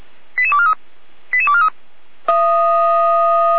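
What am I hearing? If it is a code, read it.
Five-tone selective call (ZVEI), heard as quick runs of five tones, tones of about 70 ms. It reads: 88243 88243
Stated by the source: ZVEI five-tone call 88419 received on an SDR.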